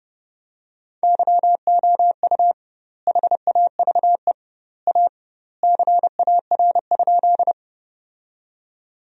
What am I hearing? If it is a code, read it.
Morse code, heard as a keyed tone, about 32 wpm, 700 Hz. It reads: YOU HAVE A CAR?
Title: 2x Word-Spacing - Sentences from Top 300 Words - 30wpm